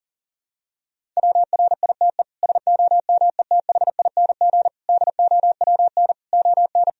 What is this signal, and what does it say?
Morse code, heard as a keyed tone, about 40 wpm, 700 Hz. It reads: WRITE SOMETHING DOWN ON
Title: Sentences from Top 1000 Words - Review Part 1 - 40wpm